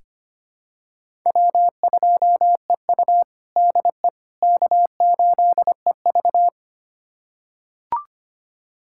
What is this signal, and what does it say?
Morse code, heard as a keyed tone, about 25 wpm, 700 Hz. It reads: W2EU DE K8EV
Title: USA Callsign DE Callsign - 4 Characters - 25wpm